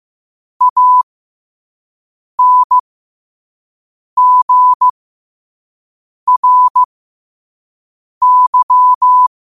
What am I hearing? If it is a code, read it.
Morse code, heard as a keyed tone, about 15 wpm, spaced as for 5 wpm, 1000 Hz. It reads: ANGRY